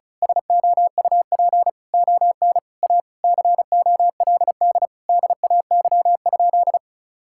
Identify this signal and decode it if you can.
Morse code, heard as a keyed tone, about 36 wpm, 700 Hz. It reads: SOUP ON A COLD DAY?
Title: Sentences from Top 1000 Words - Review Part 6 - 35wpm